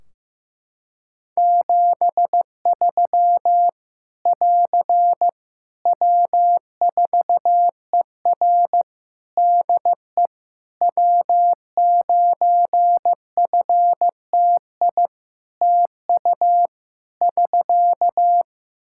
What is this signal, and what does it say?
Morse code, heard as a keyed tone, about 15 wpm, 700 Hz. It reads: 73 <AR> W4ER DE W9FTI TU <SK>
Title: Practice QSO - Exchange 4 - Ending - 15wpm